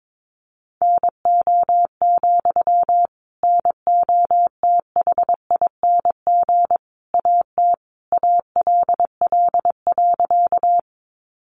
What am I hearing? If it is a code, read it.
Morse code, heard as a keyed tone, about 22 wpm, 700 Hz. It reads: NO, NOTHING AT ALL.
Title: Sentences from Top 500 Words 22wpm